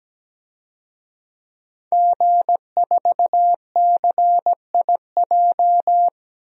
Morse code, heard as a keyed tone, about 17 wpm, 700 Hz. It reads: G4CIJ
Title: DXCC-100 Call Signs - 5 Characters - 17wpm